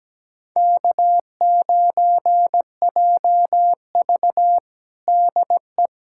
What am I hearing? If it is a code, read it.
Morse code, heard as a keyed tone, about 17 wpm, 700 Hz. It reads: K9JV DE
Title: USA Callsign DE Callsign - 4 Characters - 17wpm